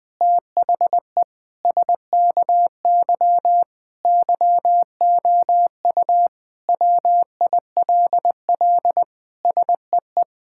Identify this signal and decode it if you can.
Morse code, heard as a keyed tone, about 21 wpm, 700 Hz. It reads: THE SKY YOU WILL SEE
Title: Sentences with 4-Word Phrases - 20wpm